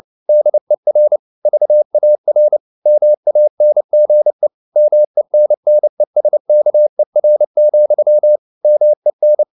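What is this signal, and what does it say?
Morse code, heard as a keyed tone, about 29 wpm, 600 Hz. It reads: DER VAR MANGE MENNESKER, MEN
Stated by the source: Fast (QRQ) Morse at 29 wpm, from CW med Gnister.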